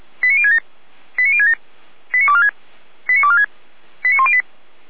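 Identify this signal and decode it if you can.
Five-tone selective call (ZVEI), heard as quick runs of five tones, tones of about 70 ms. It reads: 89067 89067 89246 89246 89108
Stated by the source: ZVEI five-tone selective call received on an SDR, from the audio archive of Feuerwehr Orlamünde.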